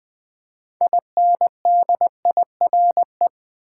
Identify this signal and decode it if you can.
Morse code, heard as a keyed tone, about 20 wpm, 700 Hz. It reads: INDIRE
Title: Build A Word - 3 Letters At A Time - 8-Letter Words - 20wpm